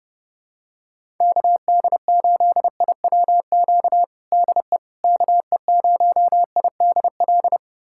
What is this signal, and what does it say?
Morse code, heard as a keyed tone, about 32 wpm, 700 Hz. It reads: KD8IWQ DE KE0IDL
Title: USA Callsign DE Callsign - 6 Characters - 30wpm